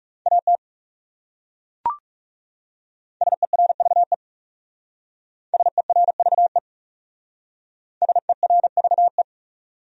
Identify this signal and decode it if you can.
Morse code, heard as a keyed tone, about 40 wpm, 700 Hz. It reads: AT SERVE SERVE SERVE